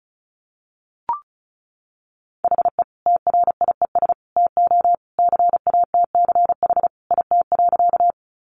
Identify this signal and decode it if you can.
Morse code, heard as a keyed tone, about 34 wpm, 700 Hz. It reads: HE TRIES TO CATCH IT.
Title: Additional Easy Phrases 35wpm